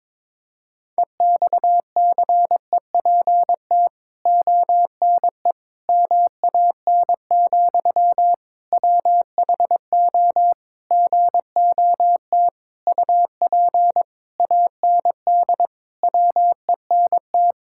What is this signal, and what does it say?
Morse code, heard as a keyed tone, about 22 wpm, 700 Hz. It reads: EXCEPT ONE MAN, WHO GOT UP AND WENT
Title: Sentences from Top 1600 Words 22wpm